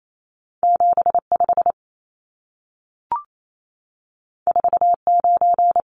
Morse code, heard as a keyed tone, about 28 wpm, 700 Hz. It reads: 75 49
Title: Numbers - 2-Digit 28wpm